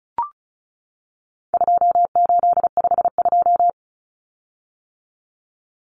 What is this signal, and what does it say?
Morse code, heard as a keyed tone, about 34 wpm, 700 Hz. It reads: 2852
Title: Common Cut Numbers - 4-Digit 35wpm